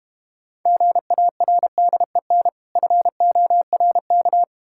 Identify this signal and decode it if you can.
Morse code, heard as a keyed tone, about 31 wpm, 700 Hz. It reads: GARDEN FORK
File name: Difficult Words - Speed-Racing with Hints - 25wpm